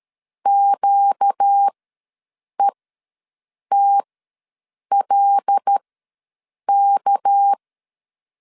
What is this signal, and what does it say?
Morse code, heard as a keyed tone, about 13 wpm, spaced as for 7 wpm, 800 Hz. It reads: QETLK
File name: LEZIONI / 6 PKX0 / 6 PKX0 ws 12wpm 3448